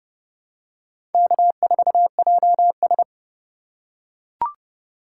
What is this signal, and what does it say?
Morse code, heard as a keyed tone, about 30 wpm, 700 Hz. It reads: K4JS
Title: USA Call Signs - 4 Characters - 4x Character Spacing - 30wpm